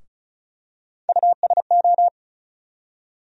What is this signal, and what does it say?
Morse code, heard as a keyed tone, about 36 wpm, 700 Hz. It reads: USO
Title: USA Call Sign Postfixes - 3 Letters 35wpm